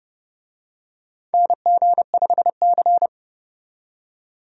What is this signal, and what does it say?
Morse code, heard as a keyed tone, about 29 wpm, 700 Hz. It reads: NG5C
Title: USA Call Signs - 4 Characters - 30wpm